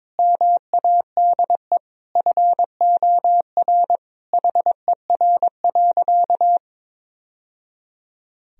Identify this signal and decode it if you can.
Morse code, heard as a keyed tone, about 22 wpm, 700 Hz. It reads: MADE FOR HER.
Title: Sentences from Top 1000 Words - Review Part 3 - 22wpm